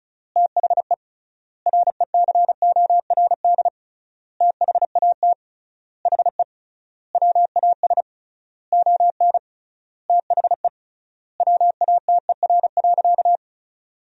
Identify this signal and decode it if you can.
Morse code, heard as a keyed tone, about 34 wpm, 700 Hz. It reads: THE RECORD THAT HE WAS ON THE WATER.
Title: 3x Word-Spacing - Sentences from Top 500 Words - 35wpm